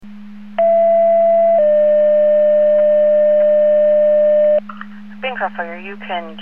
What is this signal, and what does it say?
Two-tone page, heard as a one-second tone, then a three-second tone, about 670 then 600 Hz, with speech. It is heard from 0.6 to 4.6 s.